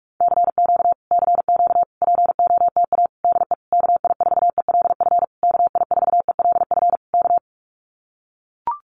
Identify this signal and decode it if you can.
Morse code, heard as a keyed tone, about 44 wpm, 700 Hz. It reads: CQ CQ POTA DE KI4ELF KI4ELF K
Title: Common QSO Phrases Encore 45wpm